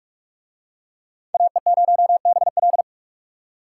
Morse code, heard as a keyed tone, about 40 wpm, 700 Hz. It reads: AE0BL